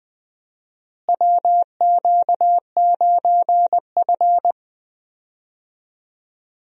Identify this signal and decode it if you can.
Morse code, heard as a keyed tone, about 20 wpm, 700 Hz. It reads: WQ9F
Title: USA Call Signs - 4 Characters - Repeated 2x - 20wpm